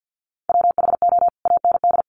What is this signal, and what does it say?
Morse code, heard as a keyed tone, about 50 wpm, 700 Hz. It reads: WHO AND
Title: Sentences from Top 100 Words 50wpm